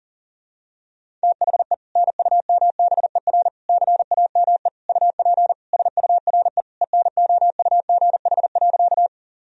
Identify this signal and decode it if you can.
Morse code, heard as a keyed tone, about 40 wpm, 700 Hz. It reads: THE NUMBER CAME UP SURE ENOUGH.
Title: Sentences from Top 1000 Words - Review Part 1 - 40wpm